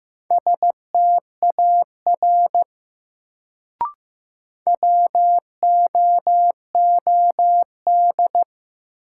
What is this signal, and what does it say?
Morse code, heard as a keyed tone, about 15 wpm, 700 Hz. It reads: STAR WOOD
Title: Sentences from Top 1000 Words - Review Part 2 - New Word Drill - 15wpm